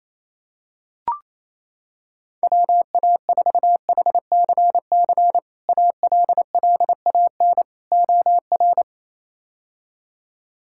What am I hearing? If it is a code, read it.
Morse code, heard as a keyed tone, about 27 wpm, 700 Hz. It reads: WA4HCC ALLAN OR
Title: K1USN SST QSO Practice - 28wpm